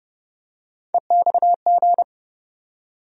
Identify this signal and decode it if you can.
Morse code, heard as a keyed tone, about 30 wpm, 700 Hz. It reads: EXG
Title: USA Call Sign Postfixes - 3 Letters 30wpm